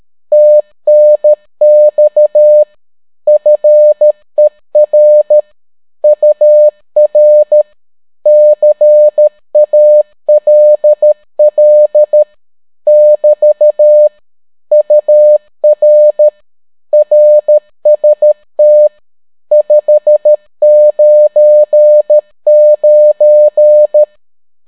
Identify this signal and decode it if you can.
Morse code, heard as a keyed tone, about 13 wpm, 600 Hz. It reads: TNX FER UR CALL <BT> UR RST 599